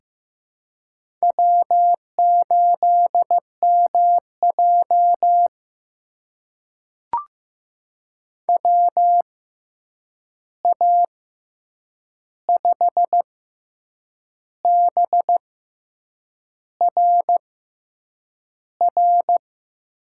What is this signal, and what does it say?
Morse code, heard as a keyed tone, about 15 wpm, spaced as for 7 wpm, 700 Hz. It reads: W8MJ WA5BRR
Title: USA Call Signs - 6x Character Spacing - 15wpm